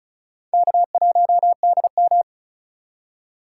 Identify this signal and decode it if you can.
Morse code, heard as a keyed tone, about 36 wpm, 700 Hz. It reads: K1DM
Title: K1USN SST QSO Practice - 35wpm